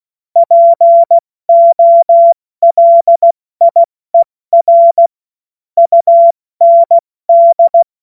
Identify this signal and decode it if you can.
Morse code, heard as a keyed tone, about 16 wpm, 680 Hz. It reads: POLIER UND
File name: Morsekurs / Witze / Witz 20. Januar 2026 70 F80